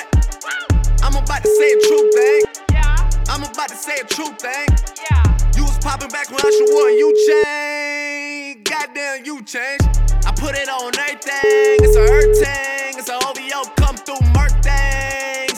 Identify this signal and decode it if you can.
Ringback tone (Europe, 425 Hz), with speech.